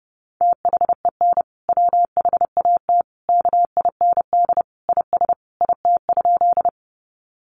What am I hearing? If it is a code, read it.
Morse code, heard as a keyed tone, about 28 wpm, 700 Hz. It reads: THEN WHAT KIND IS IT?
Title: Sentences from Top 200 Words 30wpm